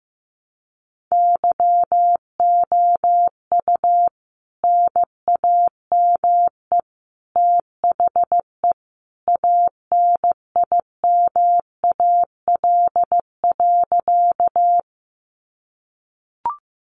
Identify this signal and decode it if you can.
Morse code, heard as a keyed tone, about 15 wpm, 700 Hz. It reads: YOU NAME THE ANIMAL.
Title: A Sound Of Thunder 15wpm